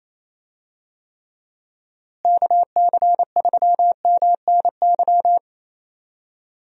Morse code, heard as a keyed tone, about 27 wpm, 700 Hz. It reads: KC3MNY
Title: USA Call Signs - 6 Characters - Repeated 2x - 28wpm